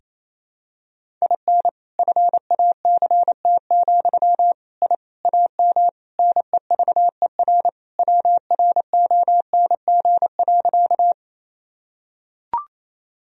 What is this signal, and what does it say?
Morse code, heard as a keyed tone, about 27 wpm, 700 Hz. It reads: IN FACT, I AM NEVER WRONG.
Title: Sentences from Top 1000 Words - Review Part 3 - 28wpm